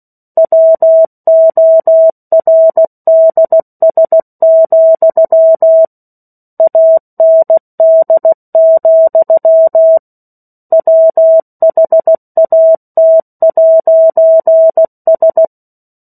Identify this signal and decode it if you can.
Morse code, heard as a keyed tone, about 16 wpm, 650 Hz. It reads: WORDS, AND, WHAT'S
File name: 03-Chapter-03 16 WPM